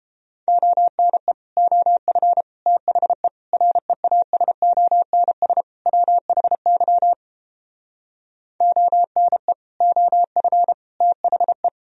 Morse code, heard as a keyed tone, about 29 wpm, 700 Hz. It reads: ONE OF THE REASONS WHY ONE OF THE